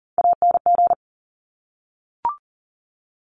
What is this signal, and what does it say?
Morse code, heard as a keyed tone, about 40 wpm, 700 Hz. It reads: ANG
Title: Top Three Letter Combinations in Words 40wpm